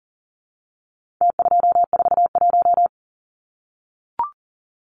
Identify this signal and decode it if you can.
Morse code, heard as a keyed tone, about 40 wpm, 700 Hz. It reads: T241
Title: Numbers - Contest Style - 4-Digit 40wpm